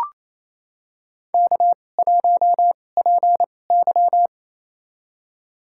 Morse code, heard as a keyed tone, about 28 wpm, spaced as for 20 wpm, 700 Hz. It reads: K1PY
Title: USA Call Signs - 4 Characters - 2x Character Spacing - 28wpm